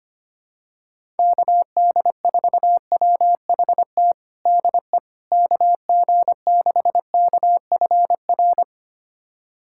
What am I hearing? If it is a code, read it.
Morse code, heard as a keyed tone, about 25 wpm, 700 Hz. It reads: KD4WHT DE KG6KFR
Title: USA Callsign DE Callsign - 6 Characters - 25wpm